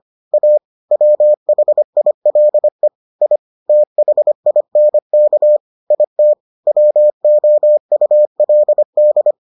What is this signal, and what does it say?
Morse code, heard as a keyed tone, about 25 wpm, 600 Hz. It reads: A WHILE I THINK IT WOULD